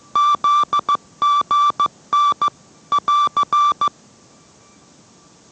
Morse code, heard as a keyed tone, about 15 wpm, 1160 Hz. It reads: ZGN <AR>